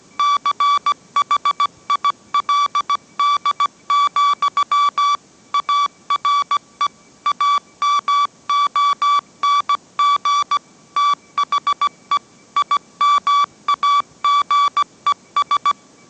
Morse code, heard as a keyed tone, about 17 wpm, 1150 Hz. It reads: CHILD, ARE AMONG THE IMAGES
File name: mrx / mrx080 18wpm